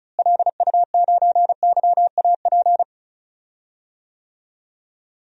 Morse code, heard as a keyed tone, about 34 wpm, 700 Hz. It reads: LU9YAP
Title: DXCC-100 Call Signs - 6 Characters - Repeated 3x - 35wpm